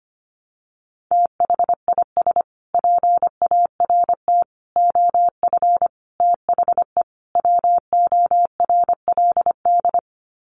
Morse code, heard as a keyed tone, about 25 wpm, 700 Hz. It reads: THIS PART OF THE WORLD
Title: Sets of 5 Words Encore2 25wpm